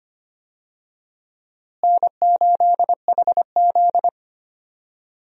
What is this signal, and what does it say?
Morse code, heard as a keyed tone, about 25 wpm, 700 Hz. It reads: N8HZ
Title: USA Call Signs - 4 Characters - Repeated 3x - 25wpm